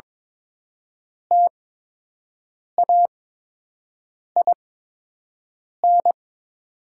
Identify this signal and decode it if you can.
Morse code, heard as a keyed tone, about 22 wpm, spaced as for 6 wpm, 700 Hz. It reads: TAIN
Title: 8x Character Spacing - Top 500 Words - 22wpm